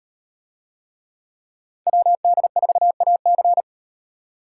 Morse code, heard as a keyed tone, about 38 wpm, 700 Hz. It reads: WD4AC